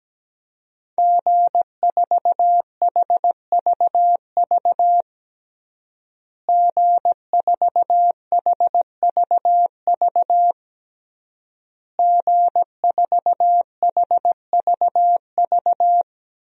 Morse code, heard as a keyed tone, about 17 wpm, 700 Hz. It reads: G4HVV G4HVV G4HVV
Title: DXCC-100 Call Signs - 5 Characters - Repeated 3x - 17wpm